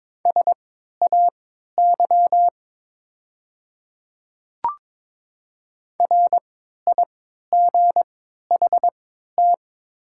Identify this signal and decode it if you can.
Morse code, heard as a keyed tone, about 22 wpm, spaced as for 13 wpm, 700 Hz. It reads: SAY RIGHT
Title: 3x Character Spacing - Top 200 Words - 22wpm